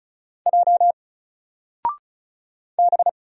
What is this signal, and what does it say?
Morse code, heard as a keyed tone, about 33 wpm, 700 Hz. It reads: J B
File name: Single-Letter-Number-35wpm